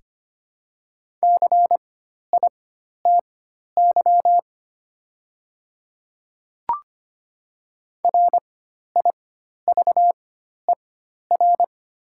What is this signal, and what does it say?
Morse code, heard as a keyed tone, about 25 wpm, spaced as for 12 wpm, 700 Hz. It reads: CITY RIVER
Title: 4x Character Spacing - Top 500 Words - 25wpm